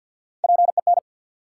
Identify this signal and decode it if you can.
Morse code, heard as a keyed tone, about 53 wpm, 700 Hz. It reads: PEN
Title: Three Letter Words - 50wpm